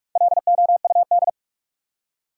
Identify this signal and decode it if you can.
Morse code, heard as a keyed tone, about 47 wpm, 700 Hz. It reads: LOUD
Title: Four Letter Words - 45wpm